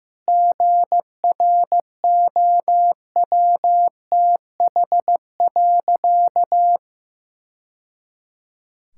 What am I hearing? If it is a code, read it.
Morse code, heard as a keyed tone, about 15 wpm, 700 Hz. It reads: GROWTH.
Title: Sentences from Top 1000 Words - Review Part 5 - 15wpm